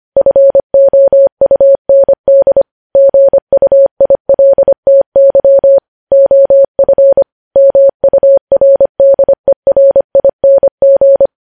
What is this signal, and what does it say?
Morse code, heard as a keyed tone, about 25 wpm, 550 Hz. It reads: FOUND GUILTY OF MURDERING
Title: News Headlines at 25